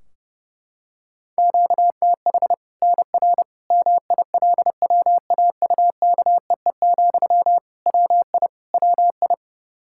Morse code, heard as a keyed tone, about 30 wpm, 700 Hz. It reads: QTH NR MILWAUKEE, WI WI
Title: K1USN SST QSO Practice 30wpm